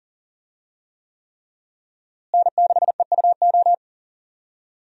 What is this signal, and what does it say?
Morse code, heard as a keyed tone, about 40 wpm, 700 Hz. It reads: N6EUO
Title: USA Call Signs - 5 Characters - Repeated 3x - 40wpm